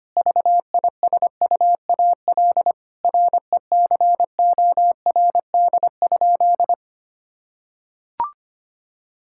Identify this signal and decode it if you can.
Morse code, heard as a keyed tone, about 25 wpm, 700 Hz. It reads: VISUAL RECORD?
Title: Sentences from Top 1000 Words - Review Part 6 - 25wpm